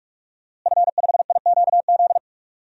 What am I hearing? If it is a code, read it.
Morse code, heard as a keyed tone, about 43 wpm, 700 Hz. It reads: F5IQZ
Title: DXCC-100 Call Signs - 5 Characters - Repeated 2x - 45wpm